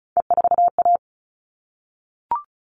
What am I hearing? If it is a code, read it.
Morse code, heard as a keyed tone, about 34 wpm, 700 Hz. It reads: E4A
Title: All Cut Numbers - 3-Digit 35wpm